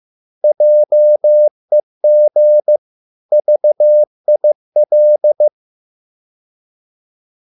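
Morse code, heard as a keyed tone, about 15 wpm, 600 Hz. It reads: JEG VIL
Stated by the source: Morse code at a slow 15 wpm from CW med Gnister.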